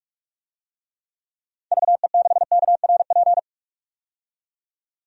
Morse code, heard as a keyed tone, about 46 wpm, 700 Hz. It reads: VE6KRP
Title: DXCC-100 Call Signs - 6 Characters - 45wpm